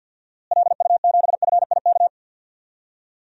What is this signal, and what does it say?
Morse code, heard as a keyed tone, about 46 wpm, 700 Hz. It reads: LU7FIK
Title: DXCC-100 Call Signs - 6 Characters - Repeated 2x - 50wpm